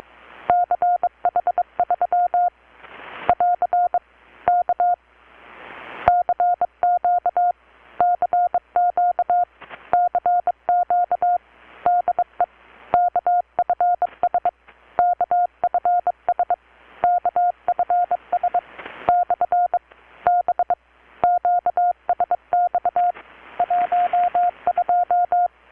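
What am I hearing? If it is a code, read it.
Morse code, heard as a keyed tone, about 23 wpm, 720 Hz. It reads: CH3 <AR> K CQ CQ CQ DE KFS KFS KFS / B QSX 12
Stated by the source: Morse from KPH's Night of Nights XX maritime radiotelegraphy event, received in Chiapas, Mexico about 3,800 km away on a Kenwood R-600 with a grounded 10 m long-wire antenna.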